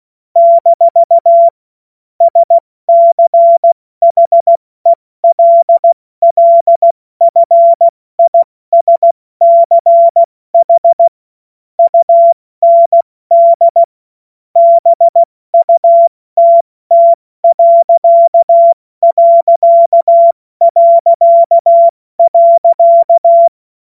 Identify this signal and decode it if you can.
Morse code, heard as a keyed tone, about 16 wpm, 680 Hz. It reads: - SCHELLFISCH UND BUTT....
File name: Morsekurs / Witze / Witz 08. August 2023 70 F80